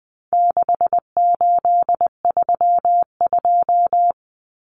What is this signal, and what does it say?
Morse code, heard as a keyed tone, about 20 wpm, 700 Hz. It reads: 6832
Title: Numbers - 4-Digit 20wpm